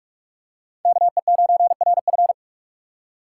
Morse code, heard as a keyed tone, about 43 wpm, 700 Hz. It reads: KE9RF
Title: USA Call Signs - 5 Characters - 3x Character Spacing - 45wpm